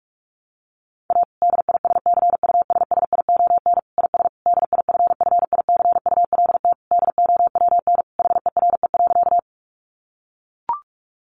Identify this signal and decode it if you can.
Morse code, heard as a keyed tone, about 44 wpm, 700 Hz. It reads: A DISCUSSION IS DIFFICULT DOWN HERE.
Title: Sentences from Top 1100 Words 45wpm